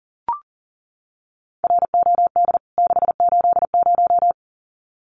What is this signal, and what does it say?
Morse code, heard as a keyed tone, about 37 wpm, 700 Hz. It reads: ROD 680
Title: CWT Exchange 40wpm